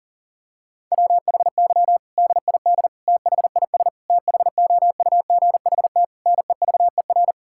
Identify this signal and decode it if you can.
Morse code, heard as a keyed tone, about 40 wpm, 700 Hz. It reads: WHY DID THIS THOUGHT NEVER
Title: Sentences from Top 1000 Words - Review Part 6 - 40wpm